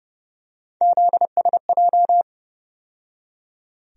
Morse code, heard as a keyed tone, about 30 wpm, 700 Hz. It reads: ZSJ
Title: USA Call Sign Postfixes - 3 Letters 30wpm